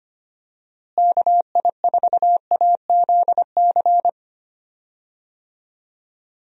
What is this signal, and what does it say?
Morse code, heard as a keyed tone, about 25 wpm, 700 Hz. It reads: KI4AZC
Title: USA Call Signs - 6 Characters - Repeated 3x - 25wpm